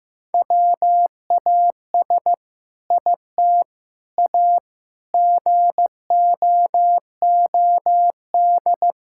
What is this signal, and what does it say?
Morse code, heard as a keyed tone, about 16 wpm, 700 Hz. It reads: WAS IT A GOOD
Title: Sentences from Top 1000 Words - Review Part 6 - 15wpm